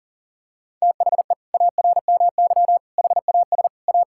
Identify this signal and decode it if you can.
Morse code, heard as a keyed tone, about 40 wpm, 700 Hz. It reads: THE ARMY HAS A
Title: Sentences from Top 1000 Words - Review Part 5 - 40wpm